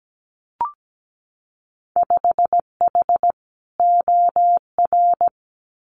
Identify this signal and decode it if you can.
Morse code, heard as a keyed tone, about 17 wpm, 700 Hz. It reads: 5H OR
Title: Winter Field Day Exchanges Encore 17wpm